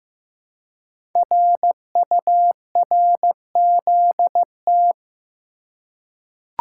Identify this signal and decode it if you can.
Morse code, heard as a keyed tone, about 15 wpm, 700 Hz. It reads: RURZT
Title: Code Groups - 5 Letters - 15wpm